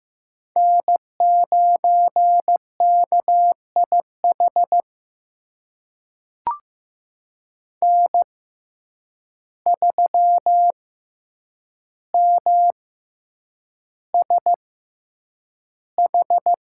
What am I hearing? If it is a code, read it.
Morse code, heard as a keyed tone, about 16 wpm, spaced as for 8 wpm, 700 Hz. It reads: N9KIH N3MSH
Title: USA Call Signs - 5 Characters - 6x Character Spacing - 15wpm